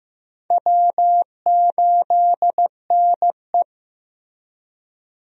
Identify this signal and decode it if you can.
Morse code, heard as a keyed tone, about 15 wpm, 700 Hz. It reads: W8NE
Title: USA Call Signs - 4 Characters - Repeated 3x - 15wpm